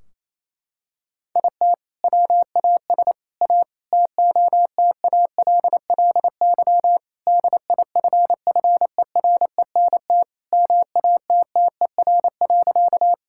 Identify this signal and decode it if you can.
Morse code, heard as a keyed tone, about 28 wpm, 700 Hz. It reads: IT WAS A TOTALLY DIFFERENT MATTER.
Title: Sentences from Top 1600 Words 28wpm